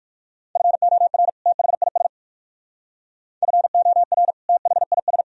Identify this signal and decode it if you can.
Morse code, heard as a keyed tone, about 54 wpm, 700 Hz. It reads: FOR THIS FOR THIS
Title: Speed-Racing - Sets of 2 Words 45wpm